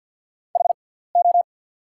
Morse code, heard as a keyed tone, about 54 wpm, spaced as for 17 wpm, 690 Hz. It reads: HO
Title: 6x Character Spacing - Top 300 Words - 50wpm